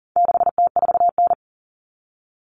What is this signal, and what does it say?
Morse code, heard as a keyed tone, about 40 wpm, 700 Hz. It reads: 6T4N
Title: Common Cut Numbers - 4-Digit 40wpm